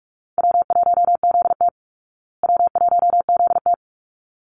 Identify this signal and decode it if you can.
Morse code, heard as a keyed tone, about 44 wpm, 700 Hz. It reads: W1ZT W1ZT
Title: Easy Call Signs Repeated 2x - 45wpm